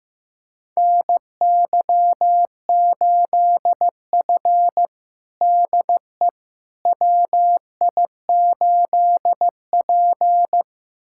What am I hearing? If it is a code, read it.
Morse code, heard as a keyed tone, about 15 wpm, 700 Hz. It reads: NY8F DE WI8P
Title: USA Callsign DE Callsign - 4 Characters - 15wpm